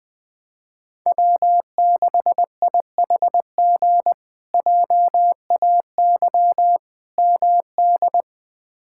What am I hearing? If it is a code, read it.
Morse code, heard as a keyed tone, about 20 wpm, 700 Hz. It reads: W6IHG JAY MD
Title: K1USN SST QSO Practice - 20wpm